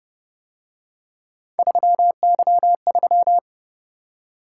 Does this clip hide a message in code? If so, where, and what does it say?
Yes: Morse code from 1.6 to 3.4 s, heard as a keyed tone, about 30 wpm, 700 Hz. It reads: 3Y3